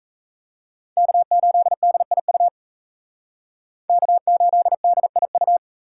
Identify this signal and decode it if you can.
Morse code, heard as a keyed tone, about 36 wpm, 700 Hz. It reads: K8DIU K8DIU